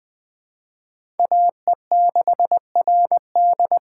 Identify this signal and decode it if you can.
Morse code, heard as a keyed tone, about 20 wpm, 700 Hz. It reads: AE6RD
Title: USA Call Signs - 5 Characters - 4x Character Spacing - 20wpm